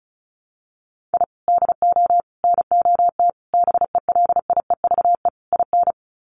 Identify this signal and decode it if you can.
Morse code, heard as a keyed tone, about 34 wpm, 700 Hz. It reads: I DO NOT BELIEVE IN
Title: Sets of 5 Words Encore 35wpm